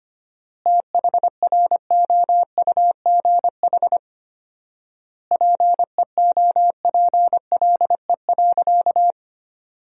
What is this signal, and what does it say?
Morse code, heard as a keyed tone, about 25 wpm, 700 Hz. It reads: THROUGH PEOPLE.
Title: 4x Word-Spacing - Sentences from Top 200 Words - 25wpm